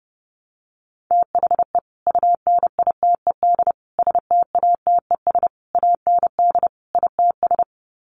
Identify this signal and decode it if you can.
Morse code, heard as a keyed tone, about 30 wpm, 700 Hz. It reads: THE UNITED STATES AND ITS
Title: Sets of 5 Words Encore 30wpm